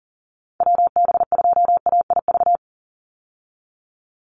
Morse code, heard as a keyed tone, about 37 wpm, 700 Hz. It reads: WB2AIV
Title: Call Signs Repeated 3x - 40wpm